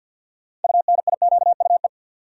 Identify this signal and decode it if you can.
Morse code, heard as a keyed tone, about 53 wpm, 700 Hz. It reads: UNIQUE
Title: Sentences from Top 1000 Words - Review Part 7 - New Word Drill - 50wpm